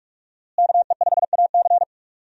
Morse code, heard as a keyed tone, about 43 wpm, 700 Hz. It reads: KE5AC
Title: USA Call Signs - 5 Characters - Repeated 2x - 45wpm